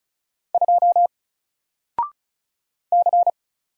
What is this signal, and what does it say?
Morse code, heard as a keyed tone, about 34 wpm, 700 Hz. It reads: 2 C
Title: Single Letter-Number 35wpm